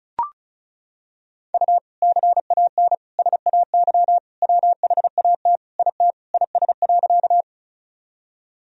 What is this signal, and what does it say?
Morse code, heard as a keyed tone, about 35 wpm, 700 Hz. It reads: U CAN SAY WHAT IT IS.
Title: Sentences from Top 200 Words and Abbreviations 35wpm